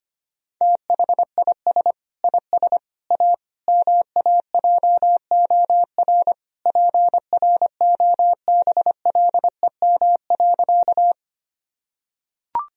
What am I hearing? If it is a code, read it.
Morse code, heard as a keyed tone, about 25 wpm, 700 Hz. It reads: THIS IS A MAJOR PROBLEM.
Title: Sentences from Top 1000 Words - Review Part 4 - 25wpm